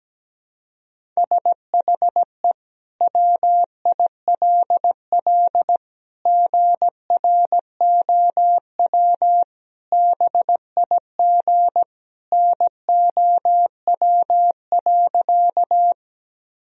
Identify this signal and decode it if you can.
Morse code, heard as a keyed tone, about 17 wpm, 700 Hz. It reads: SHE WILL GROW BIG NOW.
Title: Sentences from Top 1000 Words - Review Part 1 - 17wpm